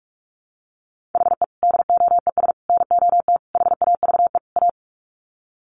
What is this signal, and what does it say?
Morse code, heard as a keyed tone, about 44 wpm, 700 Hz. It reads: HE DOES NOT HAVE A
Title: Sets of 5 Words Encore 45wpm